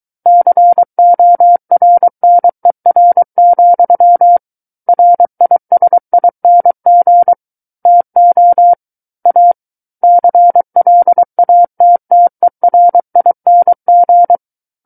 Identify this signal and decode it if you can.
Morse code, heard as a keyed tone, about 23 wpm, 700 Hz. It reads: CORNER, RISING TO A CLATTERING
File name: Book One / 14-Chapter 14 - 23 WPM